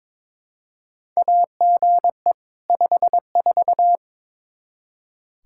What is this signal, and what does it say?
Morse code, heard as a keyed tone, about 22 wpm, 700 Hz. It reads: AGE 54
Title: Practice QSO - Exchange 3 - More Info - 22wpm